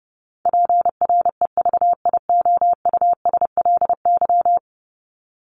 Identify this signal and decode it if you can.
Morse code, heard as a keyed tone, about 30 wpm, 700 Hz. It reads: PREVIOUSLY
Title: LY Suffix - Top 100 Words with Base Words 30wpm